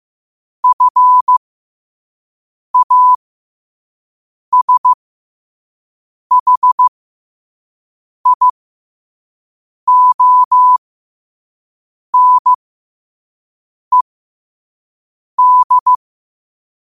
Morse code, heard as a keyed tone, about 15 wpm, spaced as for 5 wpm, 1000 Hz. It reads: FASHIONED
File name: qotdfiles / 5wpm